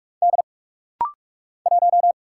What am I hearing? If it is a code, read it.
Morse code, heard as a keyed tone, about 48 wpm, 700 Hz. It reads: D 1
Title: Single Letter-Number - Rapid-Fire 45wpm